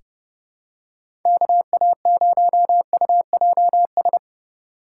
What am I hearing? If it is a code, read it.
Morse code, heard as a keyed tone, about 30 wpm, 700 Hz. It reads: KA0UJS
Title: USA Call Signs - 4x Character Spacing - 30wpm